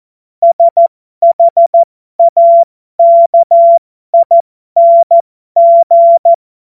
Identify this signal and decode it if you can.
Morse code, heard as a keyed tone, about 14 wpm, spaced as for 12 wpm, 680 Hz. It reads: SHAKING
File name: Morsekurs / Witze